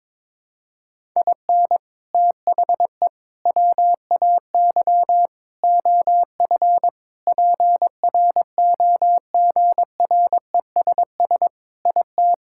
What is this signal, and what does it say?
Morse code, heard as a keyed tone, about 22 wpm, 700 Hz. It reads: IN THE WAY OF PROGRESS IT
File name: practice / 22wpm / Sentences with 4-Word Phrases - 22wpm